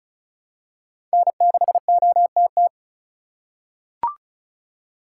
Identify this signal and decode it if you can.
Morse code, heard as a keyed tone, about 34 wpm, 700 Hz. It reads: N6OTT
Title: USA Call Signs - 5 Characters - 2x Character Spacing - 35wpm